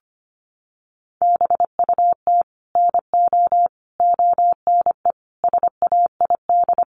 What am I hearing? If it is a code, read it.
Morse code, heard as a keyed tone, about 25 wpm, 700 Hz. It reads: BUT NO ONE SAID
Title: Sentences from Top 1000 Words 25wpm